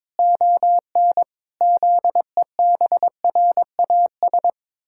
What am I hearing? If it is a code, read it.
Morse code, heard as a keyed tone, about 22 wpm, 700 Hz. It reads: ON ZEBRAS
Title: Pangram Challenge v2 - 22wpm